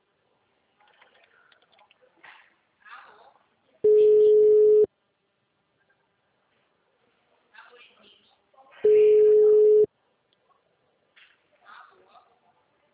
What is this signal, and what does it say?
Ringback tone (Europe, 425 Hz), with speech.